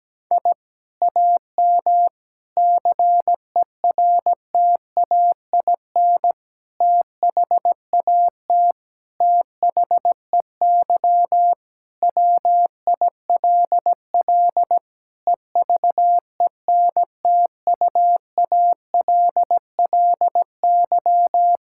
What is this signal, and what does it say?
Morse code, heard as a keyed tone, about 17 wpm, 700 Hz. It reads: I AM CERTAIN THAT THEY WILL EVENTUALLY
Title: Sentences from Top 1000 Words - Review Part 6 - 17wpm